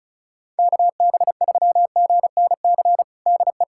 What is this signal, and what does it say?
Morse code, heard as a keyed tone, about 34 wpm, 700 Hz. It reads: KB3GNC DE
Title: USA Callsign DE Callsign - 6 Characters - 35wpm